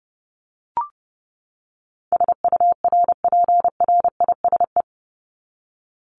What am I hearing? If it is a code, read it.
Morse code, heard as a keyed tone, about 30 wpm, 700 Hz. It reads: SURPRISE
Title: Top 1751-2000 Words 30wpm